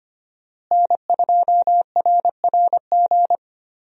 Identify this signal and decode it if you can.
Morse code, heard as a keyed tone, about 25 wpm, 700 Hz. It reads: N2RRG
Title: USA Call Signs - 5 Characters - Repeated 3x - 25wpm